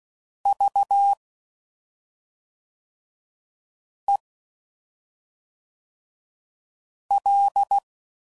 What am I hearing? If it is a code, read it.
Morse code, heard as a keyed tone, about 16 wpm, 800 Hz. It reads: V E L
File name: Suffixes / Special ING's / SlowToFaster ING's